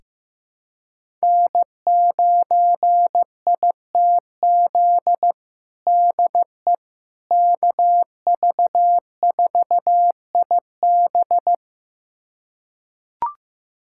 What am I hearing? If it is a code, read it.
Morse code, heard as a keyed tone, about 15 wpm, 700 Hz. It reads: N9ITZ DE KV4IB
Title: USA Callsign DE Callsign - 5 Characters - 15wpm